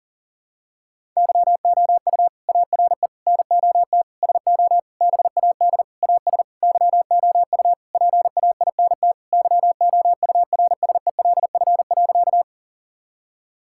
Morse code, heard as a keyed tone, about 40 wpm, 700 Hz. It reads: YOU ARE NOT SO BAD AS YOU PAINT YOURSELF.
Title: Sentences from Top 1000 Words - Review Part 6 - 40wpm